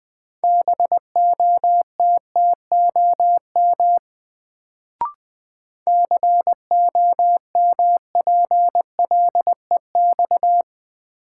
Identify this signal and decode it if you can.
Morse code, heard as a keyed tone, about 20 wpm, 700 Hz. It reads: BOTTOM COMPLEX
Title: Sentences from Top 900 Words - New Word Drill - 20wpm